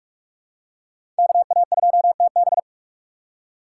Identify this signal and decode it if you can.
Morse code, heard as a keyed tone, about 44 wpm, 700 Hz. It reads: KA2TB
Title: Code Groups - 5 Letter:Numbers - 45wpm